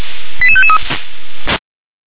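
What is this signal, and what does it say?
Five-tone selective call (ZVEI), heard as quick runs of five tones, tones of about 70 ms. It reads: 88553
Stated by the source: ZVEI five-tone selective call received on SDR.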